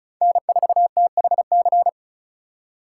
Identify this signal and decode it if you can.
Morse code, heard as a keyed tone, about 36 wpm, 700 Hz. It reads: N4THC
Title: USA Call Signs - 5 Characters - 2x Character Spacing - 35wpm